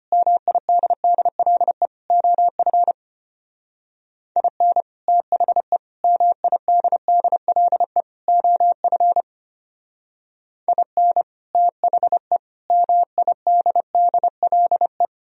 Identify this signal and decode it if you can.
Morse code, heard as a keyed tone, about 32 wpm, 700 Hz. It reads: MIDDLE OF IN THE MIDDLE OF IN THE MIDDLE